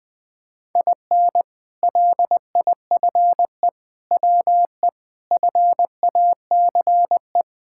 Morse code, heard as a keyed tone, about 20 wpm, 700 Hz. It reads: IN LIFE WE FACE
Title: Sentences with 7-Word Phrases - Spoken Beginning - 20wpm